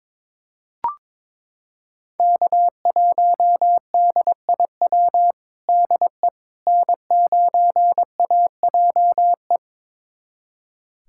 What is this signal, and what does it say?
Morse code, heard as a keyed tone, about 22 wpm, 700 Hz. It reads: K1DIW DE N9AJE
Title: USA Callsign DE Callsign - 5 Characters - 22wpm